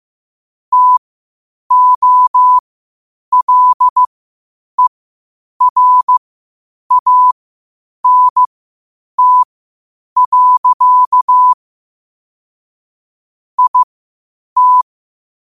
Morse code, heard as a keyed tone, about 15 wpm, spaced as for 8 wpm, 1000 Hz. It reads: TOLERANT. IT
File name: qotdfiles / 8wpm